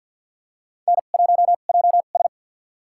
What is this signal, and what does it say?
Morse code, heard as a keyed tone, about 49 wpm, spaced as for 35 wpm, 700 Hz. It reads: N1JS
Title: USA Call Signs - 4 Characters - 2x Character Spacing - 50wpm